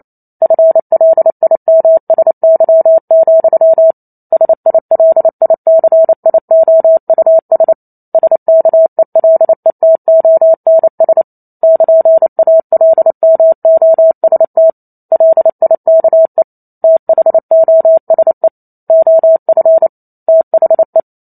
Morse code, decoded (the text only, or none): FLIMSY, SILICIOUS SKELETONS <KN>ALMOST LIKE THOSE OF THE